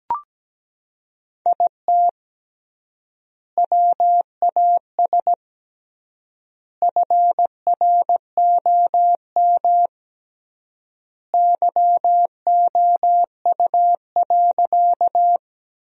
Morse code, heard as a keyed tone, about 17 wpm, 700 Hz. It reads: IT WAS FROM YOU.
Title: 3x Word-Spacing - Sentences from Top 100 Words - 17wpm